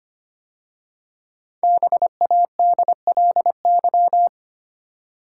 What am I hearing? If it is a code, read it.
Morse code, heard as a keyed tone, about 25 wpm, 700 Hz. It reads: BADLY
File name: 5 Letter Words - First 3 Letters Given - 25wpm